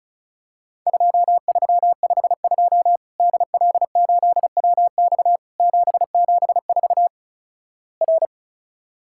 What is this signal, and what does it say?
Morse code, heard as a keyed tone, about 34 wpm, 700 Hz. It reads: 2352 DL8WX 774 R
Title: Worked All Europe DX Contest - QTC Exchange - 35wpm